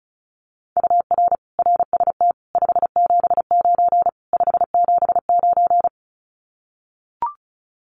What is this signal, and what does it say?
Morse code, heard as a keyed tone, about 34 wpm, 700 Hz. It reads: UR RST 579 579
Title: Common QSO Phrases Encore 35wpm